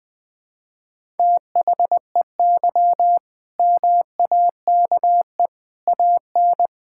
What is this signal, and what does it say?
Morse code, heard as a keyed tone, about 21 wpm, 700 Hz. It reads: THEY MAKE AN
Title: Sentences from Top 1000 Words - Review Part 6 - 20wpm